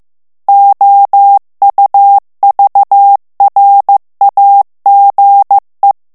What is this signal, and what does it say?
Morse code, heard as a keyed tone, about 15 wpm, 800 Hz. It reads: OUVRAGE
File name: FRANCAIS TEXTE CLAIR 2 VITESSE 14 MOTS MINUTES